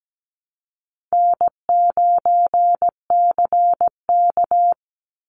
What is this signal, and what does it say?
Morse code, heard as a keyed tone, about 17 wpm, 700 Hz. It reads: N9CK
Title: Easy Call Signs Repeated 2x - 17wpm